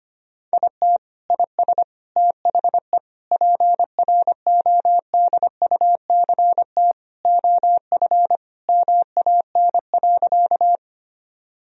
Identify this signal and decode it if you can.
Morse code, heard as a keyed tone, about 25 wpm, 700 Hz. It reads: IT IS THE PRODUCT OF MAN.
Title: Sentences from Top 1000 Words - Review Part 2 - 25wpm